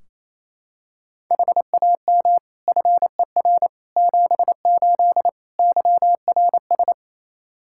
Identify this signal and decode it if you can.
Morse code, heard as a keyed tone, about 28 wpm, 700 Hz. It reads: HAM FER 78 YRS
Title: Practice QSO - Exchange 3 - More Info - 28wpm